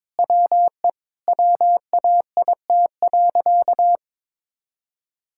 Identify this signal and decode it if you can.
Morse code, heard as a keyed tone, about 22 wpm, 700 Hz. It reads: WE WAIT.
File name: practice / 22wpm / Sentences with 3-Word Phrases - 22wpm